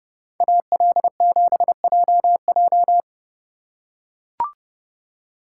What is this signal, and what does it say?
Morse code, heard as a keyed tone, about 30 wpm, 700 Hz. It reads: AL7JJ